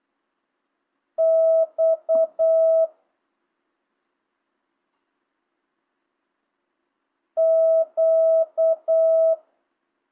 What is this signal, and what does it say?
Morse code, heard as a keyed tone, about 8 wpm, 650 Hz. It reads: X Q